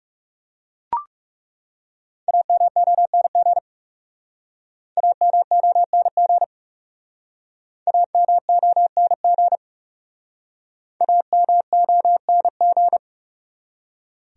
Morse code, heard as a keyed tone, about 40 wpm, 700 Hz. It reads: AMONG AMONG AMONG AMONG